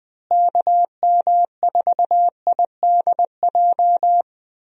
Morse code, heard as a keyed tone, about 20 wpm, 700 Hz. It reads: KM4IDJ
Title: USA Call Signs - 6 Characters - Repeated 3x - 20wpm